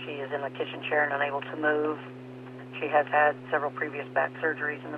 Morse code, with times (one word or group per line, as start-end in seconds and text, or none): none